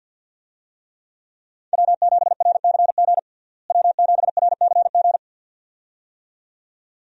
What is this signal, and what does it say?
Morse code, heard as a keyed tone, about 50 wpm, 700 Hz. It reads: W7RCG W7RCG
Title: USA Call Signs - 5 Characters - Repeated 2x - 50wpm